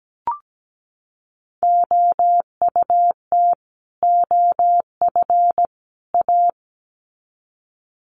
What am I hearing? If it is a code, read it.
Morse code, heard as a keyed tone, about 17 wpm, 700 Hz. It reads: OUT OF A
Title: Sets of 3 Words 17wpm